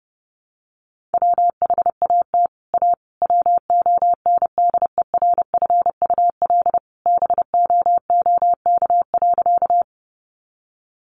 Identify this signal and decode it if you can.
Morse code, heard as a keyed tone, about 30 wpm, 700 Hz. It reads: WHAT A WONDERFUL BOOK.
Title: Sentences from Top 1100 Words 30wpm